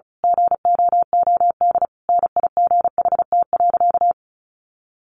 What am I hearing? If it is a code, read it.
Morse code, heard as a keyed tone, about 34 wpm, 700 Hz. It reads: GOOD NIGHT.
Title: Sentences from Top 300 Words 35wpm